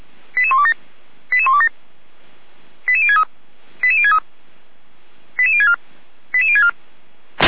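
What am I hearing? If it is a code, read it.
Five-tone selective call (ZVEI), heard as quick runs of five tones, tones of about 70 ms. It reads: 88127 88127 88063 88063 88064 88064
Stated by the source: ZVEI five-tone call received on an SDR.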